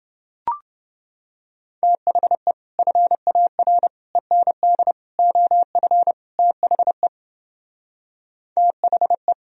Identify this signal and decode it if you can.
Morse code, heard as a keyed tone, about 28 wpm, 700 Hz. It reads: THE FAR END OF THE THE